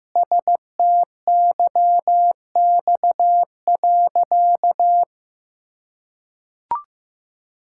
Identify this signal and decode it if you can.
Morse code, heard as a keyed tone, about 16 wpm, 700 Hz. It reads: STYX.
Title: Pangram Challenge v2 - 15wpm